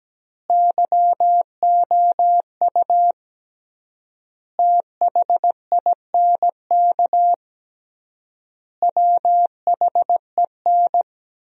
Morse code, heard as a keyed tone, about 17 wpm, 700 Hz. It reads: YOU THINK WHEN
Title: 3x Word-Spacing - Sentences from Top 300 Words - 17wpm